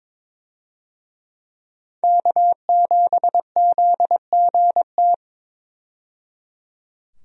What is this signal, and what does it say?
Morse code, heard as a keyed tone, about 22 wpm, 700 Hz. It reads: K7ZGT
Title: USA Call Signs - 5 Characters - 22wpm